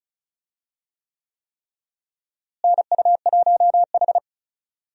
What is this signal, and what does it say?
Morse code, heard as a keyed tone, about 36 wpm, 700 Hz. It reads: NU1H
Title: USA Call Signs - 4 Characters - Repeated 3x - 35wpm